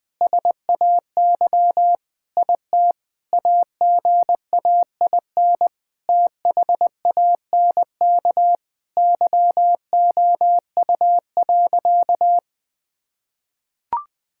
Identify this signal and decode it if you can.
Morse code, heard as a keyed tone, about 20 wpm, 700 Hz. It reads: SAY IT AGAIN THANK YOU.
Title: Sentences with 8-Word Phrases - 20wpm